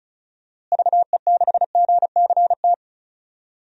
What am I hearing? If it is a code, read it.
Morse code, heard as a keyed tone, about 34 wpm, 700 Hz. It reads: VE6GCT
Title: DXCC-100 Call Signs - 6 Characters - Repeated 2x - 35wpm